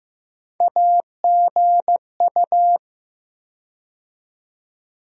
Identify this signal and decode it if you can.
Morse code, heard as a keyed tone, about 15 wpm, 700 Hz. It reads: AGU
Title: USA Call Sign Postfixes - 3 Letters 15wpm